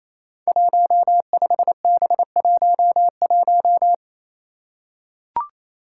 Morse code, heard as a keyed tone, about 28 wpm, 700 Hz. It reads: 15B11
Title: Five Challenging Letter-Number Combinations 28wpm